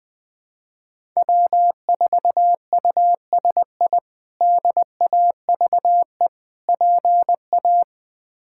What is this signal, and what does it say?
Morse code, heard as a keyed tone, about 20 wpm, 700 Hz. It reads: W4USI DAVE PA
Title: K1USN SST QSO Practice - 20wpm